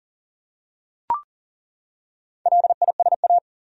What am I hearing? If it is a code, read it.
Morse code, heard as a keyed tone, about 40 wpm, 700 Hz. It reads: LISA
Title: USA Common Names - Max 4 Letters - 40wpm